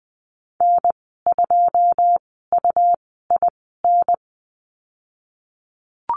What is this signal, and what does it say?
Morse code, heard as a keyed tone, about 19 wpm, spaced as for 14 wpm, 700 Hz. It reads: N2UIN
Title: 147 Lesson 24g - Letter y - Call Signs 20wpm